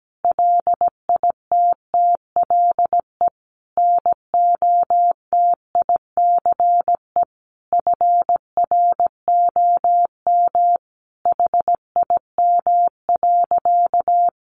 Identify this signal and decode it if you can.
Morse code, heard as a keyed tone, about 17 wpm, 700 Hz. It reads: LITTLE NOTICE FROM HIM.
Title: Sentences from Top 400 Words 17wpm